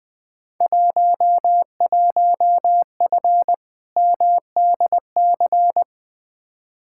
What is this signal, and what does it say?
Morse code, heard as a keyed tone, about 20 wpm, 700 Hz. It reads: 11F MDC
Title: Field Day Exchanges Encore - 20wpm